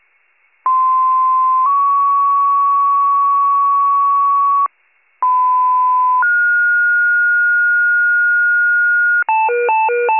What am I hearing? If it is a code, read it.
Two-tone page, heard as a one-second tone, then a three-second tone, about 1040 then 1120 Hz, about 1010 then 1430 Hz. It is heard from 0.7 to 9.2 s.
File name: pager / 95 RESCUE / wcalert